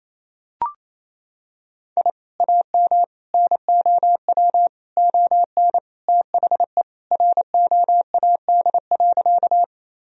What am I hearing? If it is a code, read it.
Morse code, heard as a keyed tone, about 28 wpm, 700 Hz. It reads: I AM NOW ON THE ROAD.
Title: Sentences from Top 1000 Words - Review Part 2 - 28wpm